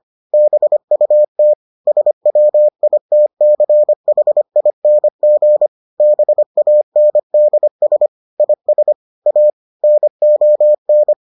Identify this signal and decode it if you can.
Morse code, heard as a keyed tone, about 25 wpm, 600 Hz. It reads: BUT SWITCHING BANDS IS A NON